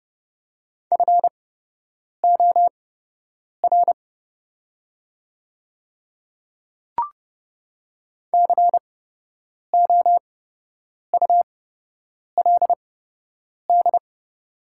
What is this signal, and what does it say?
Morse code, heard as a keyed tone, about 30 wpm, spaced as for 8 wpm, 700 Hz. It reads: FOR COULD
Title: 8x Character Spacing - Top 100 Words - 30wpm